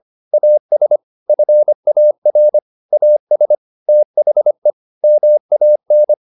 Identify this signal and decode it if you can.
Morse code, heard as a keyed tone, about 25 wpm, 600 Hz. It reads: AS FAR AS THE MAN
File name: the old man and the sea part 1 25wpm